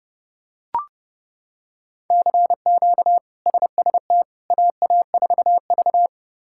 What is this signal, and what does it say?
Morse code, heard as a keyed tone, about 30 wpm, 700 Hz. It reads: CQ SST AA4V
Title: K1USN SST QSO Practice - 30wpm